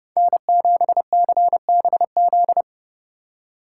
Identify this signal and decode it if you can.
Morse code, heard as a keyed tone, about 40 wpm, 700 Hz. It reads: N7CBZ